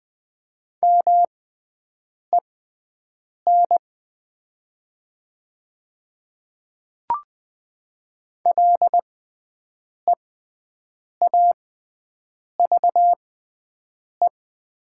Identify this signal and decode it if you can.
Morse code, heard as a keyed tone, about 20 wpm, spaced as for 7 wpm, 700 Hz. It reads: MEN LEAVE
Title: 6x Character Spacing - Top 500 Words - 20wpm